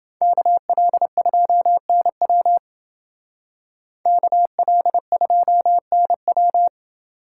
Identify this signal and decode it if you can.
Morse code, heard as a keyed tone, about 29 wpm, 700 Hz. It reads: KL2NW KL2NW